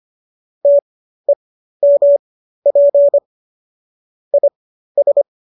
Morse code, heard as a keyed tone, about 25 wpm, spaced as for 13 wpm, 570 Hz. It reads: TEMP IS